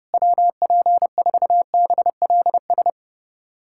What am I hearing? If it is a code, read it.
Morse code, heard as a keyed tone, about 31 wpm, 700 Hz. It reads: WP4BLS
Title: USA Callsign DE Callsign - 6 Characters - 30wpm